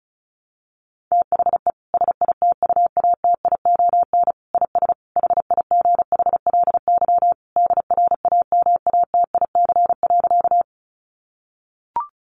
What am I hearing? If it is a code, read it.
Morse code, heard as a keyed tone, about 34 wpm, 700 Hz. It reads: THE SITUATION IS HIGHLY DRAMATIC.
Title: Sentences from Top 1300 Words 35wpm